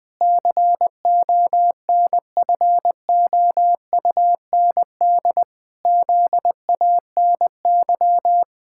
Morse code, heard as a keyed tone, about 20 wpm, 700 Hz. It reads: CONFOUND ZANY
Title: Pangram Challenge v2 - 20wpm